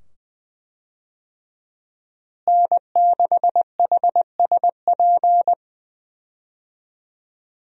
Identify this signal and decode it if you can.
Morse code, heard as a keyed tone, about 20 wpm, 700 Hz. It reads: N6HSP
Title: USA Call Signs - 5 Characters - Repeated 2x - 20wpm